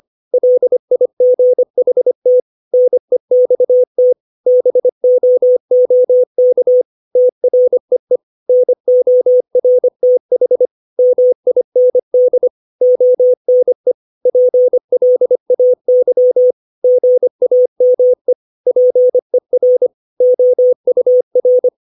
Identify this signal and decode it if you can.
Morse code, heard as a keyed tone, about 25 wpm, 500 Hz. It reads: LIGHT NEXT BOOK TREE NORTH MIND ONE PLAY GAME PER OUR
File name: 632 words / 632 Words 25WPM